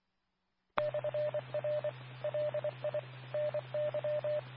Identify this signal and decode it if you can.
Morse code, heard as a keyed tone, about 24 wpm, 600 Hz, with radio noise. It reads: /R LI NY
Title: Nassau and Suffolk County Amateur Repeaters